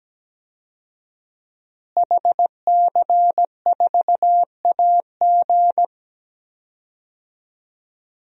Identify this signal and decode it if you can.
Morse code, heard as a keyed tone, about 17 wpm, 700 Hz. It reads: HC4AG
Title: DXCC-100 Call Signs - 5 Characters - Repeated 2x - 17wpm